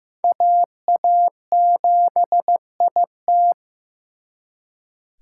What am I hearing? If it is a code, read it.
Morse code, heard as a keyed tone, about 18 wpm, 700 Hz. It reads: AA7IT
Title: Speed Racing - USA 2x2 Call Signs 15wpm